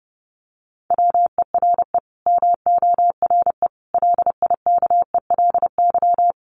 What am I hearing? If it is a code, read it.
Morse code, heard as a keyed tone, about 30 wpm, 700 Hz. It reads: WERE MORE LIKELY
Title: Sets of 5 Words Encore2 30wpm